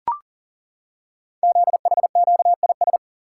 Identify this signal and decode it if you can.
Morse code, heard as a keyed tone, about 40 wpm, 700 Hz. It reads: ZHQIS
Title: Code Groups - 5 Letters - 40wpm